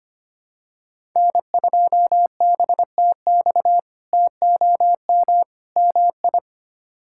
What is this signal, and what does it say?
Morse code, heard as a keyed tone, about 25 wpm, 700 Hz. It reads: N2BTX TOM MI
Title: K1USN SST QSO Practice - 25wpm